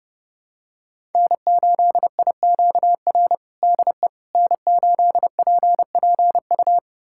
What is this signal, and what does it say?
Morse code, heard as a keyed tone, about 30 wpm, 700 Hz. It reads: N8IQR DE N8PPU
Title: USA Callsign DE Callsign - 5 Characters - 30wpm